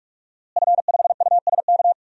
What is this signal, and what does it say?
Morse code, heard as a keyed tone, about 44 wpm, 700 Hz. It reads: F5USK
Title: DXCC-100 Call Signs - 5 Characters - Repeated 2x - 45wpm